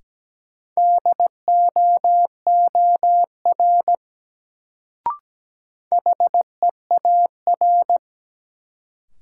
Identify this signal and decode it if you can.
Morse code, heard as a keyed tone, about 17 wpm, 700 Hz. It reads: DOOR HEAR
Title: Sentences from Top 300 Words - New Word Drill - 17wpm